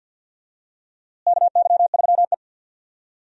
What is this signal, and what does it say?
Morse code, heard as a keyed tone, about 52 wpm, 700 Hz. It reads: KY3E
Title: USA Call Signs - 4 Characters - Repeated 2x - 50wpm